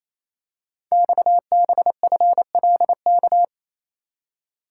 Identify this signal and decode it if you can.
Morse code, heard as a keyed tone, about 28 wpm, 700 Hz. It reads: XBFLK
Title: Code Groups - 5 Letters - 28wpm